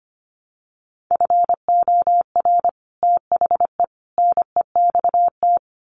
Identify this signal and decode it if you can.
Morse code, heard as a keyed tone, about 25 wpm, 700 Hz. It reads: FOR THE NEXT
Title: Sets of 3 Words 25wpm